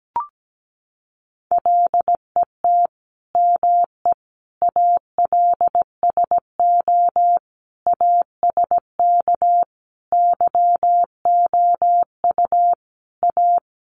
Morse code, heard as a keyed tone, about 17 wpm, 700 Hz. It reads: LET ME ALSO ASK YOU A
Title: Sentences from Top 400 Words 17wpm